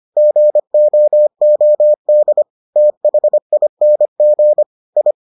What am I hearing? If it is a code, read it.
Morse code, heard as a keyed tone, about 25 wpm, 600 Hz. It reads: GOOD THING I